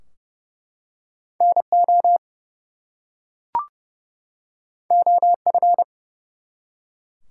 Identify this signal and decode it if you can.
Morse code, heard as a keyed tone, about 30 wpm, 700 Hz. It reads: NO OF
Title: Two Letter Words - 30wpm